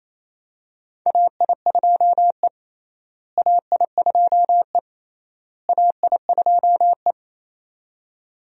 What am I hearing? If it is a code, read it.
Morse code, heard as a keyed tone, about 28 wpm, 700 Hz. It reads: AI2E AI2E AI2E
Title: USA Call Signs - 4 Characters - Repeated 3x - 28wpm